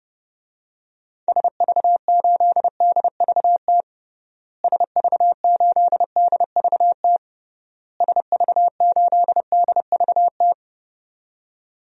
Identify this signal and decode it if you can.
Morse code, heard as a keyed tone, about 30 wpm, 700 Hz. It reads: SV8DVT SV8DVT SV8DVT
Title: DXCC-100 Call Signs - 6 Characters - Repeated 3x - 30wpm